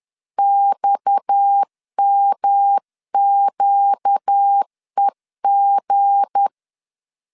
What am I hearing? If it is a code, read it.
Morse code, heard as a keyed tone, about 10 wpm, 800 Hz. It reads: XMQEG
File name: LEZIONI / 7 GF9Z / 7 G 10wpm 2501